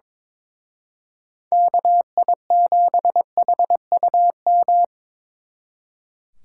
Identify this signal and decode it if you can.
Morse code, heard as a keyed tone, about 26 wpm, 700 Hz. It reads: KI7HUM
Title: Speed Racing - USA 2x3 Call Signs 22wpm